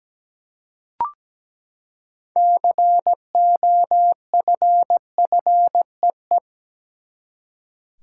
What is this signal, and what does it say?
Morse code, heard as a keyed tone, about 17 wpm, 700 Hz. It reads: COFFEE
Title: Six Letter Words - 17wpm